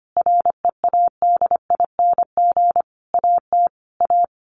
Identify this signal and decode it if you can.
Morse code, heard as a keyed tone, about 25 wpm, 700 Hz. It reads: READING AT A
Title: Sentences from Top 900 Words 25wpm